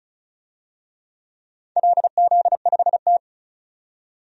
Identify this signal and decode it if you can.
Morse code, heard as a keyed tone, about 36 wpm, 690 Hz. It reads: LZ5T
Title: DXCC-100 Call Signs - 4 Characters - 35wpm